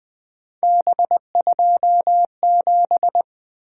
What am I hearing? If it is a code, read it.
Morse code, heard as a keyed tone, about 24 wpm, 700 Hz. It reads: B27
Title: Speed-Racing - Three Letter-Number Combinations - 20wpm